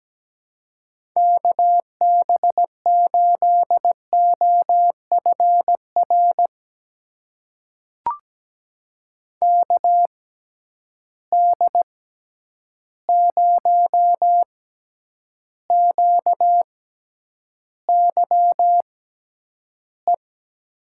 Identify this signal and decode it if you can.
Morse code, heard as a keyed tone, about 17 wpm, spaced as for 9 wpm, 700 Hz. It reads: KB8OFR KD0QYE
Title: USA Call Signs - 6 Characters - 6x Character Spacing - 17wpm